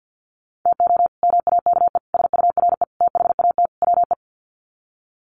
Morse code, heard as a keyed tone, about 50 wpm, 700 Hz. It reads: TO MAKE SURE THAT WE
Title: Sets of 5 Words Encore 50wpm